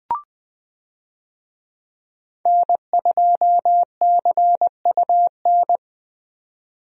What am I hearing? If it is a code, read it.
Morse code, heard as a keyed tone, about 21 wpm, 700 Hz. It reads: N2CUN
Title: USA Call Signs - 5 Characters - 20wpm